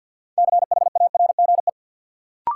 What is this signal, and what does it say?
Morse code, heard as a keyed tone, about 53 wpm, 700 Hz. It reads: CHARGE
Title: Sentences from Top 900 Words - New Word Drill - 50wpm